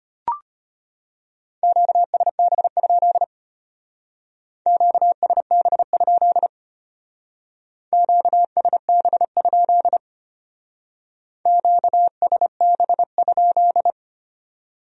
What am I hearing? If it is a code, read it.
Morse code, heard as a keyed tone, about 32 wpm, 700 Hz. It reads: QSB? QSB? QSB? QSB?